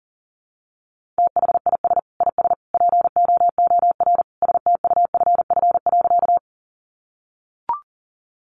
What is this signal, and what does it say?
Morse code, heard as a keyed tone, about 40 wpm, 700 Hz. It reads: THIS IS POOR STUFF.
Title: Sentences from Top 1400 Words 40wpm